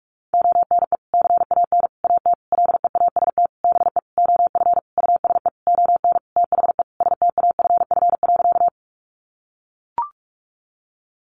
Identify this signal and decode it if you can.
Morse code, heard as a keyed tone, about 44 wpm, 700 Hz. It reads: ONE CAN AT LEAST BE OF USE ON THE STAFF.
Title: Sentences from Top 1200 Words 45wpm